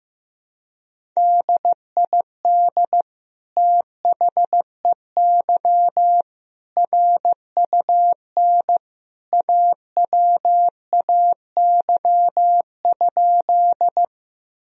Morse code, heard as a keyed tone, about 15 wpm, 700 Hz. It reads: DID THEY RUN AWAY?
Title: Sentences from Top 1000 Words - Review Part 1 - 15wpm